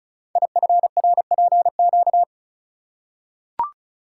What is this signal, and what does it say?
Morse code, heard as a keyed tone, about 36 wpm, 700 Hz. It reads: IFRPQ
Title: Code Groups - 5 Letter:Numbers - 35wpm